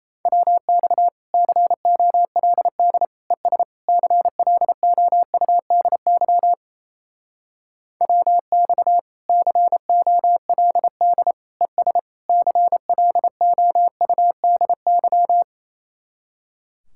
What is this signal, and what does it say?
Morse code, heard as a keyed tone, about 38 wpm, 700 Hz. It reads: WX COLD ES CLOUDY WX COLD ES CLOUDY